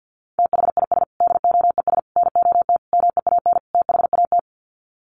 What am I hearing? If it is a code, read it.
Morse code, heard as a keyed tone, about 50 wpm, 700 Hz. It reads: THIS DOES NOT MEAN THAT
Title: Sets of 5 Words Encore2 50wpm